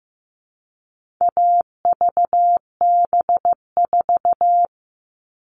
Morse code, heard as a keyed tone, about 15 wpm, 700 Hz. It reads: AVB4
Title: All Cut Numbers - 4-Digit 15wpm